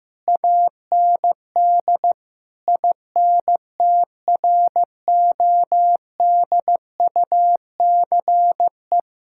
Morse code, heard as a keyed tone, about 16 wpm, 700 Hz. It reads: AND INTRODUCE
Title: Sentences from Top 1600 Words 15wpm